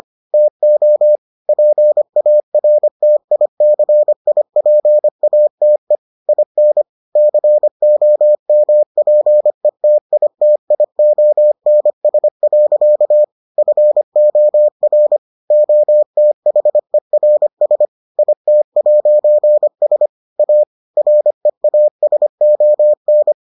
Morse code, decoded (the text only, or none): TO PARTICIPATE IN COMPETITIONS. FOR OTHERS IT'S A REASON